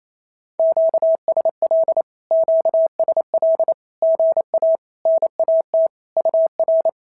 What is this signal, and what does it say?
Morse code, heard as a keyed tone, about 28 wpm, 650 Hz. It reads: QSL QSL GA NAT UR
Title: SOTA QSO Practice 28wpm